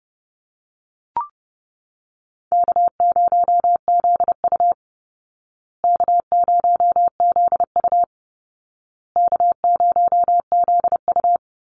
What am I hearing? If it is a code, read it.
Morse code, heard as a keyed tone, about 29 wpm, 700 Hz. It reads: K0ZU K0ZU K0ZU
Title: Easy Call Signs Repeated 3x - 30wpm